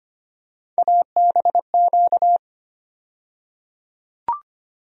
Morse code, heard as a keyed tone, about 25 wpm, 700 Hz. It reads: ABQ
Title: USA Call Sign Postfixes - 3 Letters 25wpm